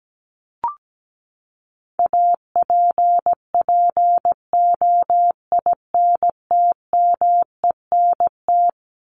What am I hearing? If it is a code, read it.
Morse code, heard as a keyed tone, about 17 wpm, 700 Hz. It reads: APPOINTMENT
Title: Top 1751-2000 Words 17wpm